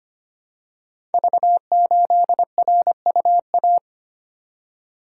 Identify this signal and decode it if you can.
Morse code, heard as a keyed tone, about 25 wpm, 700 Hz. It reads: V8RUA
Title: Code Groups - 5 Letter:Numbers - 25wpm